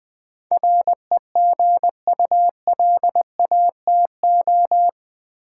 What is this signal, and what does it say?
Morse code, heard as a keyed tone, about 21 wpm, 700 Hz. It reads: REGULATO
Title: Build A Word - 4 Letters At A Time - 10-Letter Words - 20wpm